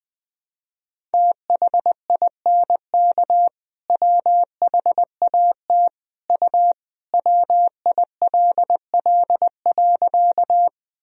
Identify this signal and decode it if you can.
Morse code, heard as a keyed tone, about 20 wpm, 700 Hz. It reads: THINK WHAT U WILL.
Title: Sentences from Top 200 Words and Abbreviations 20wpm